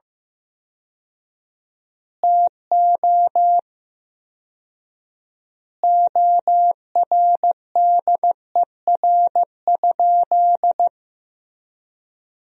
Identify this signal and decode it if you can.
Morse code, heard as a keyed tone, about 15 wpm, 700 Hz. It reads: TO ORDER?
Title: 4x Word-Spacing - Sentences from Top 400 Words - 15wpm